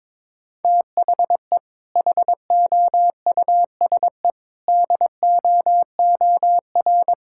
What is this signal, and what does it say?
Morse code, heard as a keyed tone, about 22 wpm, 700 Hz. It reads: THE HOUSE DOOR
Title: Sentences from Top 1000 Words - Review Part 6 - 22wpm